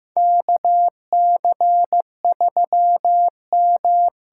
Morse code, heard as a keyed tone, about 15 wpm, 700 Hz. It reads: KC3M
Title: USA Call Signs - 8x Character Spacing - 15wpm